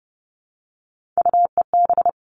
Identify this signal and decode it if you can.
Morse code, heard as a keyed tone, about 30 wpm, 700 Hz. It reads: UEB
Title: All Cut Numbers - 3-Digit 30wpm